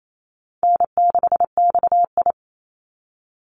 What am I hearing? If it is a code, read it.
Morse code, heard as a keyed tone, about 28 wpm, 700 Hz. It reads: N6XI
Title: Easy Call Signs - 28wpm